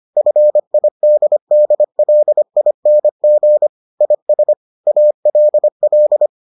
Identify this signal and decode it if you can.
Morse code, heard as a keyed tone, about 25 wpm, 600 Hz. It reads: FIDDLING IS ALL